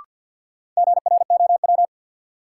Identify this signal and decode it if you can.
Morse code, heard as a keyed tone, about 47 wpm, 700 Hz. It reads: GROW
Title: Sentences from Top 300 Words - New Word Drill - 50wpm